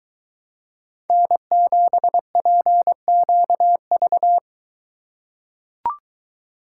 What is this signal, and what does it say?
Morse code, heard as a keyed tone, about 22 wpm, 700 Hz. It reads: N7PQV